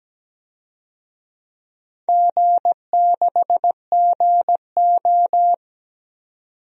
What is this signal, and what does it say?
Morse code, heard as a keyed tone, about 17 wpm, 700 Hz. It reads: G6GO
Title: DXCC-100 Call Signs - 4 Characters - 17wpm